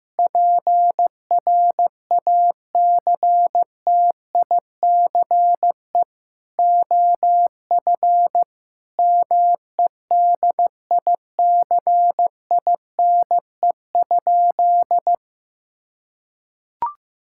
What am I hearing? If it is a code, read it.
Morse code, heard as a keyed tone, about 16 wpm, 700 Hz. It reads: PRACTICE OF MEDICINE?
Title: Sentences from Top 1000 Words - Review Part 6 - 15wpm